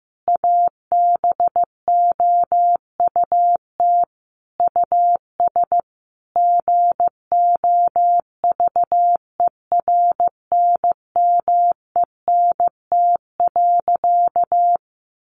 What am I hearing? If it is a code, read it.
Morse code, heard as a keyed tone, about 15 wpm, 700 Hz. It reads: ABOUT US GOVERNMENT.
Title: Easy Phrases 15wpm